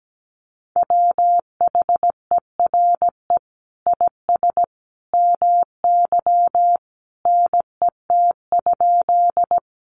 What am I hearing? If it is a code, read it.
Morse code, heard as a keyed tone, about 17 wpm, 700 Hz. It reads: WHERE IS MY NET?
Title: Sentences from Top 1400 Words 17wpm